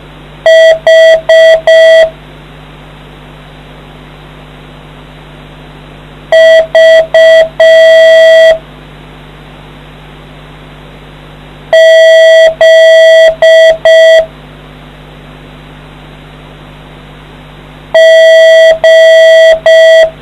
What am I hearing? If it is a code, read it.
Morse code, heard as a keyed tone, about 6 wpm, 650 Hz. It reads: H V Z G